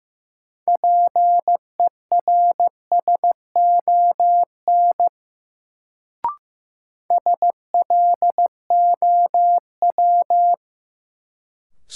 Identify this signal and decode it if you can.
Morse code, heard as a keyed tone, about 15 wpm, 700 Hz. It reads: PERSON SLOW
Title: Sentences from Top 400 Words - New Word Drill - 15wpm